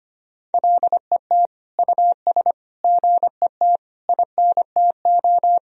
Morse code, heard as a keyed tone, about 25 wpm, 700 Hz. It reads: LET US GET INTO
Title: Sentences from Top 1000 Words - Review Part 4 - 25wpm